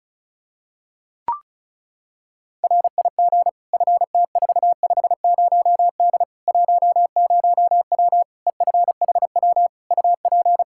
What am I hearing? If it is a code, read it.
Morse code, heard as a keyed tone, about 36 wpm, 700 Hz. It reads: RIG FT450D 10W EFHW UP
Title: K1USN SST QSO Practice 35wpm